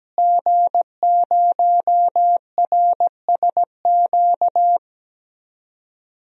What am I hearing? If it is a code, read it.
Morse code, heard as a keyed tone, about 17 wpm, 700 Hz. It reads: G0RSQ
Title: DXCC-100 Call Signs - 5 Characters - Repeated 3x - 17wpm